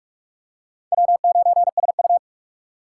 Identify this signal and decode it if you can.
Morse code, heard as a keyed tone, about 45 wpm, 700 Hz. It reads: W9SU